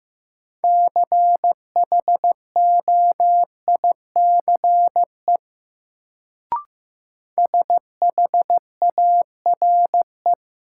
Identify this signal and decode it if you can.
Morse code, heard as a keyed tone, about 15 wpm, 700 Hz. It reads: CHOICE SHARE
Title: Sentences from Top 900 Words - New Word Drill - 15wpm